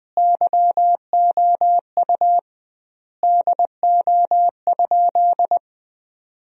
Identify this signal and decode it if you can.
Morse code, heard as a keyed tone, about 20 wpm, 700 Hz. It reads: YOU DO?
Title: 2x Word-Spacing - Sentences from Top 100 Words - 20wpm